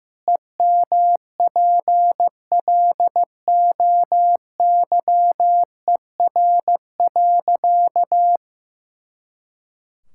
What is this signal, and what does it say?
Morse code, heard as a keyed tone, about 15 wpm, 700 Hz. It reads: EMPLOYER.
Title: Pangram Challenge v2 - 15wpm